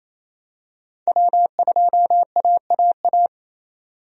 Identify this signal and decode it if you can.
Morse code, heard as a keyed tone, about 27 wpm, 700 Hz. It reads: W2AAA
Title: USA Call Signs - 5 Characters - 28wpm